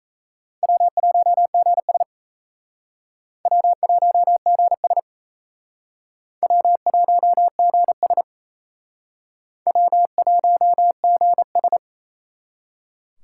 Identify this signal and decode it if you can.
Morse code, heard as a keyed tone, about 38 wpm, 700 Hz. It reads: W1GS W1GS W1GS W1GS